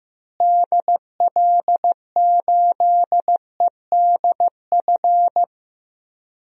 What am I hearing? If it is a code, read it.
Morse code, heard as a keyed tone, about 15 wpm, 700 Hz. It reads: DL8EDF
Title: DXCC-100 Call Signs - 6 Characters - Repeated 3x - 15wpm